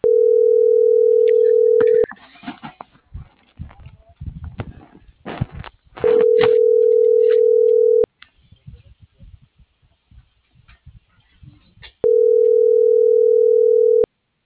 Ringback tone (North America), 440 + 480 Hz.